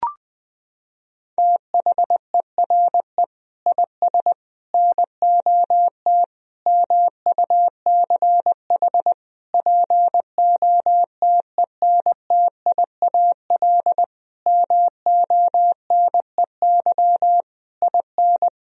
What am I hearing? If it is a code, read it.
Morse code, heard as a keyed tone, about 20 wpm, 700 Hz. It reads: THERE IS NOT MUCH POTENTIAL MONEY IN
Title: Sentences from Top 1000 Words - Review Part 4 - 20wpm